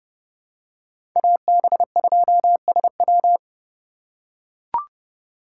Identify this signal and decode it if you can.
Morse code, heard as a keyed tone, about 30 wpm, 700 Hz. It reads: AB2SW
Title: USA Call Signs - 5 Characters - 2x Character Spacing - 30wpm